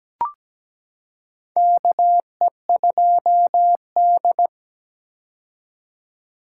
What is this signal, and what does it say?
Morse code, heard as a keyed tone, about 17 wpm, 700 Hz. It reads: KE2D
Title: K1USN SST QSO Practice - 17wpm